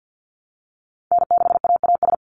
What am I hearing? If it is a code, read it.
Morse code, heard as a keyed tone, about 50 wpm, 700 Hz. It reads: N6AAS
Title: USA Call Signs 50wpm